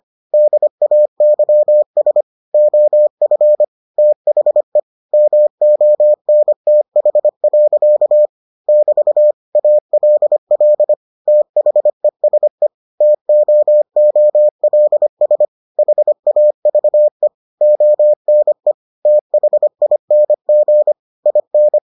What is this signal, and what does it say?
Morse code, heard as a keyed tone, about 25 wpm, 600 Hz. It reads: DAYS OF THE MONTH. <BT> ALL THESE TOOLS HAVE ONE THING IN